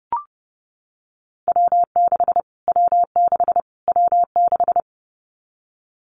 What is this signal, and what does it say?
Morse code, heard as a keyed tone, about 30 wpm, 700 Hz. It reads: W6 W6 W6
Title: Callsign Prefixes for Top 25 DXCC - Repeated 3x - 30wpm